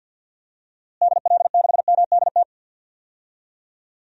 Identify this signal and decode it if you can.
Morse code, heard as a keyed tone, about 53 wpm, 700 Hz. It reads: DL6MDT
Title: DXCC-100 Call Signs - 6 Characters - Repeated 2x - 50wpm